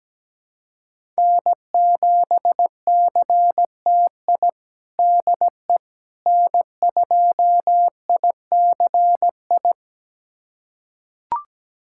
Morse code, heard as a keyed tone, about 17 wpm, 700 Hz. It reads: N7CTI DE N2ICI
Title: USA Callsign DE Callsign - 5 Characters - 17wpm